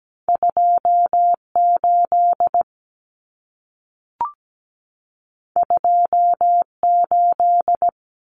Morse code, heard as a keyed tone, about 17 wpm, 700 Hz. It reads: 28 28
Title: Numbers - 2-Digit 17wpm